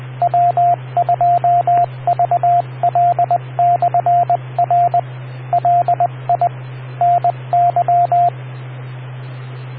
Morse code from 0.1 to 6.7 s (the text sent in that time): W2VL/R LI